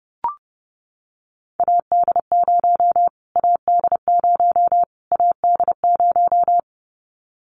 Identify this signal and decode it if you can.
Morse code, heard as a keyed tone, about 30 wpm, 700 Hz. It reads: AD0 AD0 AD0
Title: Callsign Prefixes for Top 25 DXCC - Repeated 3x - 30wpm